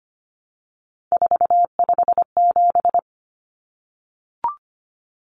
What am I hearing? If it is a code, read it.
Morse code, heard as a keyed tone, about 25 wpm, 700 Hz. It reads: 457
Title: Common Cut Numbers - 3-Digit 25wpm